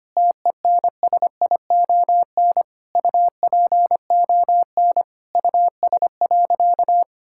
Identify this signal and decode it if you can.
Morse code, heard as a keyed tone, about 25 wpm, 700 Hz. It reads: TENSION UPON US.
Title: Sentences from Top 1600 Words 25wpm